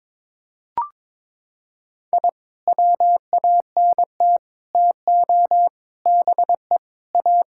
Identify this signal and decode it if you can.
Morse code, heard as a keyed tone, about 21 wpm, 700 Hz. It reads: I WANT TO BE A
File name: practice / 22wpm / Sentences with 3-Word Phrases - 22wpm